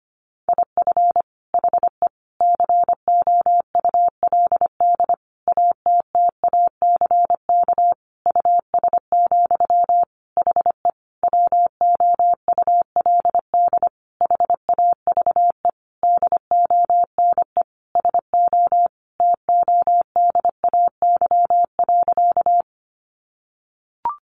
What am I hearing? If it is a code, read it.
Morse code, heard as a keyed tone, about 25 wpm, 700 Hz. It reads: IF HE COULD ATTACK US, HE WOULD HAVE DONE SO TODAY.
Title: Sentences from Top 1500 Words 25wpm